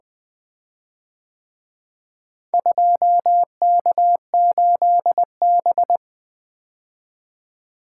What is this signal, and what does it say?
Morse code, heard as a keyed tone, about 24 wpm, 700 Hz. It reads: 2K8B
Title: Speed-Racing - Four Letter-Number Combinations - 20wpm